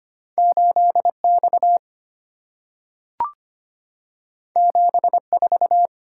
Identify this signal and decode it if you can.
Morse code, heard as a keyed tone, about 25 wpm, 700 Hz. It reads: 8X 74
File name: Two Letter-Number Combinations - 25wpm